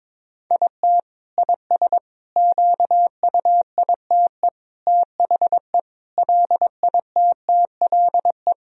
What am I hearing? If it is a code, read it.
Morse code, heard as a keyed tone, about 22 wpm, 700 Hz. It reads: IT IS QUITE THE LITTLE
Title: Sentences from Top 1000 Words - Review Part 2 - 22wpm